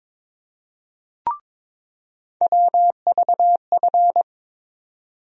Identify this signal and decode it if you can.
Morse code, heard as a keyed tone, about 22 wpm, 700 Hz. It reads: WVF
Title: USA Call Sign Postfixes - 3 Letters 22wpm